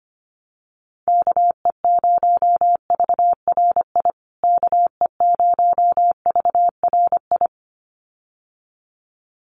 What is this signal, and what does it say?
Morse code, heard as a keyed tone, about 25 wpm, 700 Hz. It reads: KE0VRI KE0VRI
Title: USA Call Signs Repeated 2x 25wpm